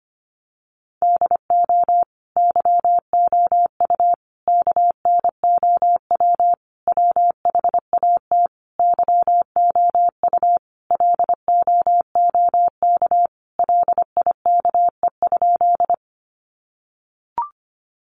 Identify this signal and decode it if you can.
Morse code, heard as a keyed tone, about 25 wpm, 700 Hz. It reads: DO YOU KNOW WHAT YOU LOOK LIKE?
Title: Sentences from Top 100 Words 25wpm